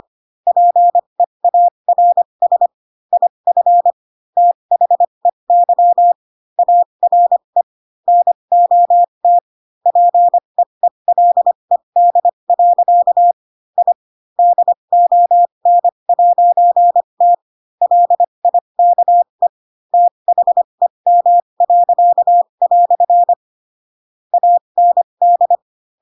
Morse code, heard as a keyed tone, about 25 wpm, spaced as for 22 wpm, 700 Hz. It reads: PEARS IF THEY ARE NOT PEELED. I DON'T LIKE THEM." AND
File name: Pinocchio / Pinocchio ch7 18 25 ik1hge